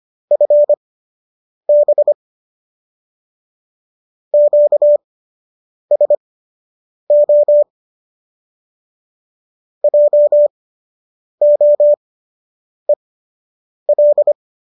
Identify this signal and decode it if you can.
Morse code, heard as a keyed tone, about 25 wpm, spaced as for 8 wpm, 590 Hz. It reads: FB QSO JOEL